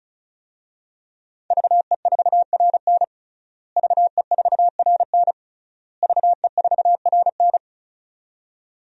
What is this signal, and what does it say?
Morse code, heard as a keyed tone, about 34 wpm, 700 Hz. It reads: VE4RN VE4RN VE4RN
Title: DXCC-100 Call Signs - 5 Characters - Repeated 3x - 35wpm